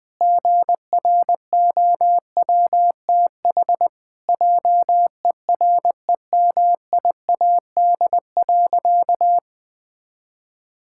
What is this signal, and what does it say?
Morse code, heard as a keyed tone, about 20 wpm, 700 Hz. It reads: GROWTH JEREMIAD.
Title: Pangram Challenge v2 - 20wpm